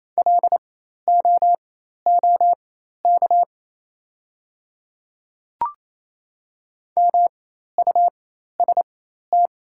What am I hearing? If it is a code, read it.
Morse code, heard as a keyed tone, about 28 wpm, spaced as for 13 wpm, 700 Hz. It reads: LOOK MUST
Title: 4x Character Spacing - Top 300 Words - 28wpm